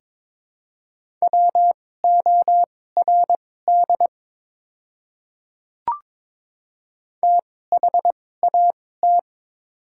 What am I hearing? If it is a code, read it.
Morse code, heard as a keyed tone, about 23 wpm, spaced as for 16 wpm, 690 Hz. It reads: WORD THAT
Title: 2x Character Spacing - Top 100 Words - 22wpm